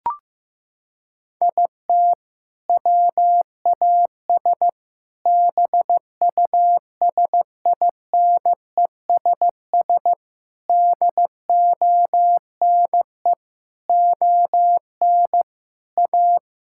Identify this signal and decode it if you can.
Morse code, heard as a keyed tone, about 15 wpm, 690 Hz. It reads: IT WAS BUSINESS DONE ON A
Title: Sentences from Top 1000 Words - Review Part 5 - 15wpm